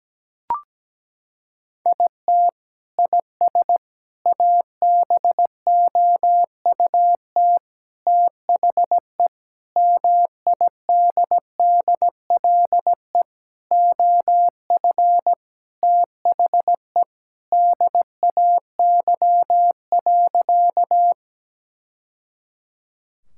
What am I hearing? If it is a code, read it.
Morse code, heard as a keyed tone, about 17 wpm, 700 Hz. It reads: IT IS ABOUT THE MIDDLE OF THE DAY.
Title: Sentences from Top 1000 Words - Review Part 6 - 17wpm